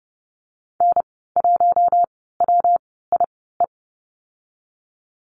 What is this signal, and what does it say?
Morse code, heard as a keyed tone, about 30 wpm, spaced as for 17 wpm, 700 Hz. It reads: N1WIE
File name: Lesson 19g - Letter w - Call Signs 30wpm